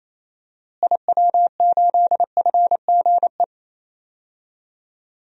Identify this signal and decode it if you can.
Morse code, heard as a keyed tone, about 28 wpm, 700 Hz. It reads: IW8FGE
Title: DXCC-100 Call Signs - 6 Characters - Repeated 2x - 28wpm